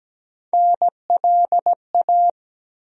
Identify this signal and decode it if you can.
Morse code, heard as a keyed tone, about 17 wpm, 700 Hz. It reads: NLA
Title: USA Call Sign Postfixes - 3 Letters 17wpm